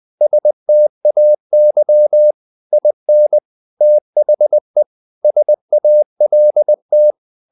Morse code, heard as a keyed tone, about 20 wpm, 600 Hz. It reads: STAY IN THE SALT
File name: the old man and the sea part 2 20wpm